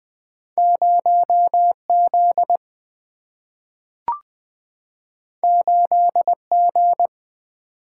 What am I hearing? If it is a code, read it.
Morse code, heard as a keyed tone, about 20 wpm, 700 Hz. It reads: 0Z 8G
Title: Single-Letter Single-Number 20wpm